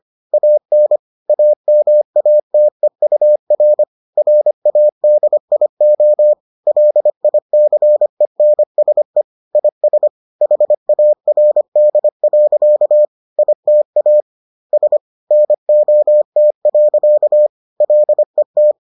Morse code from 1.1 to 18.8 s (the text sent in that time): AMATEUR RADIO LICENSE IS HARD. ITA S NOT. LET